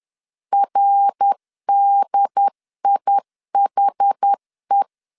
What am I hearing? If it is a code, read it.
Morse code, heard as a keyed tone, about 10 wpm, 800 Hz. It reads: RDIHE
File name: CW / DET / 1 3 1500C S A 10wpm 3447